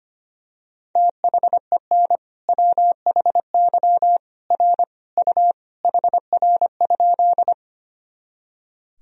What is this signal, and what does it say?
Morse code, heard as a keyed tone, about 25 wpm, 700 Hz. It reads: THEN WHY R U HR?
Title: Sentences from Top 200 Words and Abbreviations 25wpm